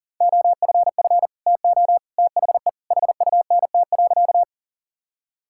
Morse code, heard as a keyed tone, about 37 wpm, 700 Hz. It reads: OFF TO THE HUNT.
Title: Sentences from Top 1600 Words 40wpm